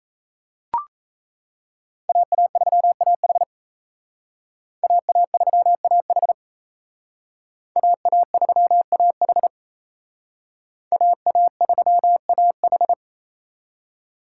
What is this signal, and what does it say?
Morse code, heard as a keyed tone, about 34 wpm, 700 Hz. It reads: AA3AH AA3AH AA3AH AA3AH
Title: Speed Racing - USA 2x2 Call Signs 28wpm